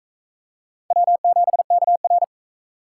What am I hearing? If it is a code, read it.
Morse code, heard as a keyed tone, about 41 wpm, 700 Hz. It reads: W7KR